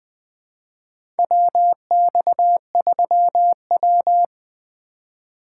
Morse code, heard as a keyed tone, about 20 wpm, 700 Hz. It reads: WX3W
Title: USA Call Signs - 4 Characters - 2x Character Spacing - 20wpm